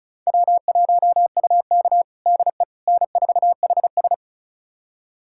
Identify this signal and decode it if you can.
Morse code, heard as a keyed tone, about 34 wpm, 690 Hz. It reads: W1UK DE N4HS
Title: USA Callsign DE Callsign - 4 Characters - 35wpm